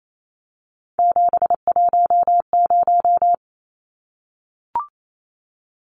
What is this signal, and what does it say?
Morse code, heard as a keyed tone, about 28 wpm, 700 Hz. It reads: 710
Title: Numbers - 3-Digit 28wpm